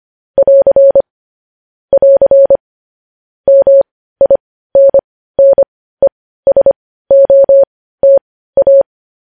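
Morse code, heard as a keyed tone, about 25 wpm, spaced as for 15 wpm, 550 Hz. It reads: <AR> <AR> MINNESOTA
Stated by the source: Morse code news headlines sent at 15 wpm.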